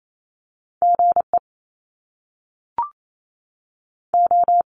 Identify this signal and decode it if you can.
Morse code, heard as a keyed tone, about 28 wpm, 700 Hz. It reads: GE O
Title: Top Two Letter Combinations in Words 28wpm